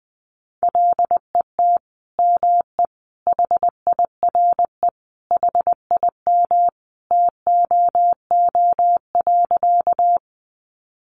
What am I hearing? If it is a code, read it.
Morse code, heard as a keyed tone, about 19 wpm, 700 Hz. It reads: LET ME HIRE HIM TOO.
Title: Sentences from Top 1400 Words 20wpm